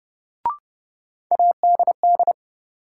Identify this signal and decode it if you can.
Morse code, heard as a keyed tone, about 32 wpm, 700 Hz. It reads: ADD
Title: Sentences from Top 1000 Words - Review Part 1 - New Word Drill - 30wpm